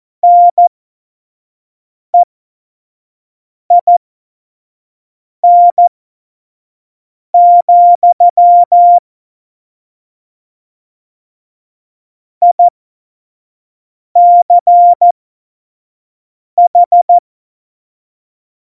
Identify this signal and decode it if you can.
Morse code, heard as a keyed tone, about 14 wpm, spaced as for 5 wpm, 700 Hz. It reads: NEIN, ICH
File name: Morsekurs / Witze / Witz 20. August 2019, 25 F70